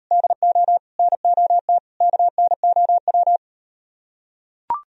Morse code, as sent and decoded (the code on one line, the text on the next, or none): -.. --- / -. --- - / -.- -. --- .--
DO NOT KNOW